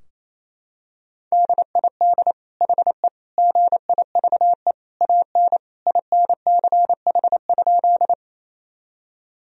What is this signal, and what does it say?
Morse code, heard as a keyed tone, about 28 wpm, 700 Hz. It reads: DID HE GIVE AN INCH?
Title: Sentences from Top 1000 Words - Review Part 3 - 28wpm